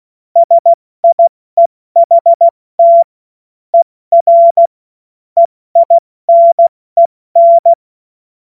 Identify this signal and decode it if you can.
Morse code, heard as a keyed tone, about 16 wpm, 680 Hz. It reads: SIEHT ER EINEN
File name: Morsekurs / Witze / Witz 20. September 2022 70 F80